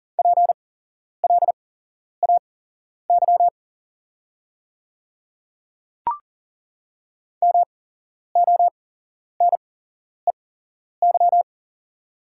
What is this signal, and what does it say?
Morse code, heard as a keyed tone, about 40 wpm, spaced as for 11 wpm, 690 Hz. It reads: PLAY MONEY
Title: 8x Character Spacing - Top 500 Words - 40wpm